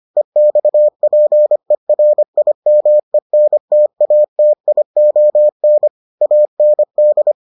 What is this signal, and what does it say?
Morse code, heard as a keyed tone, about 25 wpm, 600 Hz. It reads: EXPERIMENTATION AND